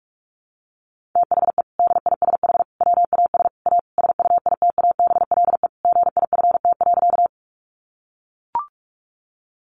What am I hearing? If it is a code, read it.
Morse code, heard as a keyed tone, about 44 wpm, 700 Hz. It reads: THE DISH WAS A SUITABLE GIFT.
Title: Sentences from Top 1400 Words 45wpm